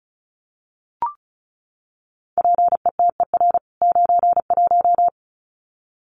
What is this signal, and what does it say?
Morse code, heard as a keyed tone, about 34 wpm, 700 Hz. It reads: PETER 91
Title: CWT Exchange 35wpm